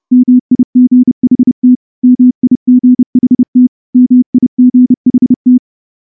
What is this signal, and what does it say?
Morse code, heard as a keyed tone, about 29 wpm, 250 Hz. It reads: MIGHT MIGHT MIGHT
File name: morse-training-30WPM